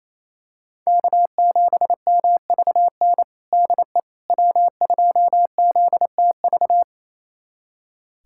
Morse code, heard as a keyed tone, about 28 wpm, 700 Hz. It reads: K7MVN DE W2ZTV
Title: USA Callsign DE Callsign 28wpm